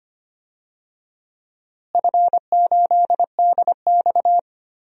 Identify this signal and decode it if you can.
Morse code, heard as a keyed tone, about 25 wpm, 700 Hz. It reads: F8DX
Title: DXCC-100 Call Signs - 4 Characters - Repeated 2x - 25wpm